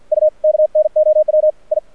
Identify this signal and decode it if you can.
Morse code, heard as a keyed tone, about 48 wpm, 620 Hz. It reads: U KNOW I
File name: archived / CRSnet